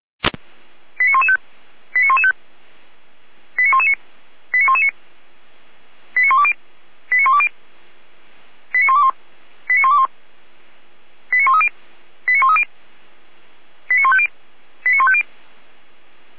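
Five-tone selective call (ZVEI), heard as quick runs of five tones, tones of about 70 ms. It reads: 89115 89115 89119 89119 89120 89120 89121 89121 89130 89130 89150 89150
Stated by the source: ZVEI five-tone call 89110, received on SDR.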